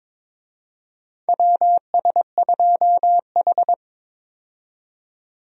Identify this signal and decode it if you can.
Morse code, heard as a keyed tone, about 22 wpm, 700 Hz. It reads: WS2H
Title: USA Call Signs - 4 Characters - 22wpm